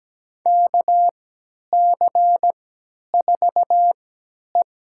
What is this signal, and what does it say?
Morse code, heard as a keyed tone, about 17 wpm, spaced as for 10 wpm, 700 Hz. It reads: KC4E
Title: USA Call Signs - 3x Character Spacing - 17wpm